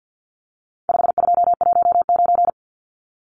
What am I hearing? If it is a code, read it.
Morse code, heard as a keyed tone, about 50 wpm, 700 Hz. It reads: 5219
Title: Numbers - 4-Digit 50wpm